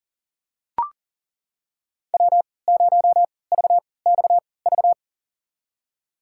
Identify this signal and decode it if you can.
Morse code, heard as a keyed tone, about 40 wpm, spaced as for 23 wpm, 700 Hz. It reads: W0VXV
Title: USA Call Signs - 5 Characters - 3x Character Spacing - 40wpm